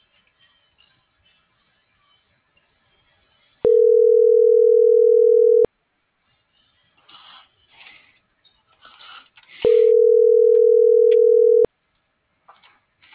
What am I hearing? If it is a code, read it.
Ringback tone (North America), 440 + 480 Hz.